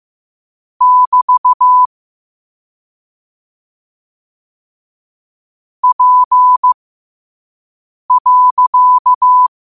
Morse code, heard as a keyed tone, about 15 wpm, spaced as for 5 wpm, 1000 Hz. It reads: <BT> P.
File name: qotdfiles / 5wpm